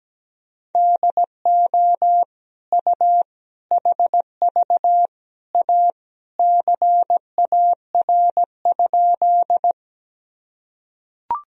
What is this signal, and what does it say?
Morse code, heard as a keyed tone, about 17 wpm, 700 Hz. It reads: DO U HV A CAR?
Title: Sentences from Top 300 Words and Abbreviations 17wpm